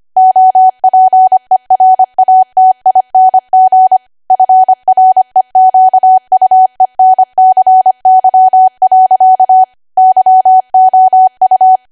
Morse code, heard as a keyed tone, about 25 wpm, 750 Hz. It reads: OPERATING FREQUENCY. YOU
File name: Archive / 25 WPM / 25WPM